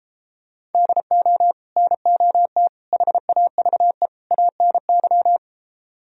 Morse code, heard as a keyed tone, about 33 wpm, 700 Hz. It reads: DO NOT HAVE ANY